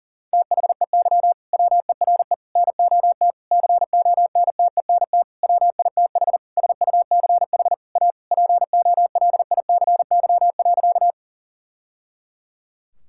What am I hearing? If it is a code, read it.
Morse code, heard as a keyed tone, about 40 wpm, 700 Hz. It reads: THEY WERE NOT CONTENT WITH SUCH A POLICY.
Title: Sentences from Top 1000 Words - Review Part 6 - 40wpm